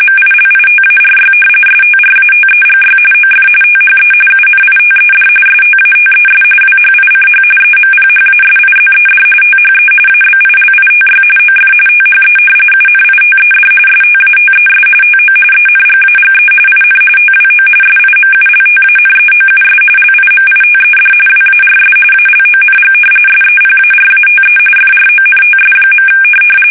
RTTY, about 45.45 baud. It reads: MILITARY STRATEGY OF THE UNITED STATES WHICH EMPHASIZED THE ABILITY OF THE UNITED STATES TO ENGAGE IN TWO SIMULTANEOUS LIMITED WARS ALONG SEPARATE FRONTS.
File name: fsk850 / usnavy5